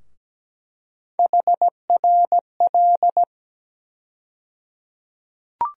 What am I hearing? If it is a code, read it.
Morse code, heard as a keyed tone, about 17 wpm, 700 Hz. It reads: HRL
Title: USA Call Sign Postfixes - 3 Letters 17wpm